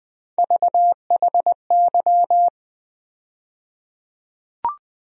Morse code, heard as a keyed tone, about 21 wpm, 700 Hz. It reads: VHY
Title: USA Call Sign Postfixes - 3 Letters 20wpm